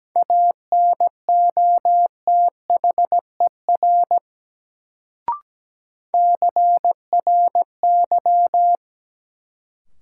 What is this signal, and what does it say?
Morse code, heard as a keyed tone, about 17 wpm, 700 Hz. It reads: ANOTHER CRY
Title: Sets of 5 Words Encore - New Word Drill - 17wpm